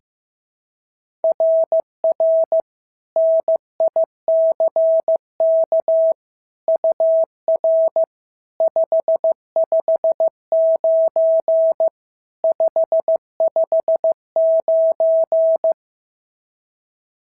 Morse code, heard as a keyed tone, about 15 wpm, 690 Hz. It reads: RR NICK UR 559 559
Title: SOTA QSO Practice 15wpm